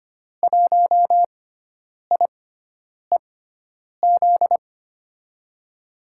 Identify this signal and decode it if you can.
Morse code, heard as a keyed tone, about 25 wpm, spaced as for 9 wpm, 700 Hz. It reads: 1IEZ
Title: USA Call Signs - 6 Characters - 6x Character Spacing - 25wpm